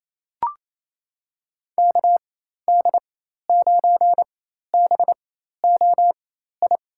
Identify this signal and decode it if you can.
Morse code, heard as a keyed tone, about 28 wpm, spaced as for 13 wpm, 700 Hz. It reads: KD9BOI
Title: USA Call Signs - 4x Character Spacing - 28wpm